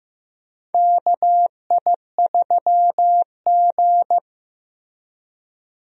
Morse code, heard as a keyed tone, about 15 wpm, 700 Hz. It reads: KI3G
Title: USA Call Signs - 4 Characters - Repeated 3x - 15wpm